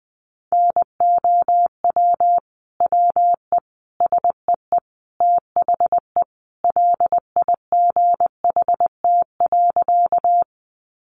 Morse code, heard as a keyed tone, about 20 wpm, 700 Hz. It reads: NOW WE SEE THE LIGHT.
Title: Sentences from Top 200 Words 20wpm